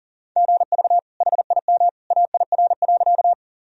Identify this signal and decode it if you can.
Morse code, heard as a keyed tone, about 40 wpm, 700 Hz. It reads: GV HIM AIR.
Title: Sentences from Top 200 Words and Abbreviations 40wpm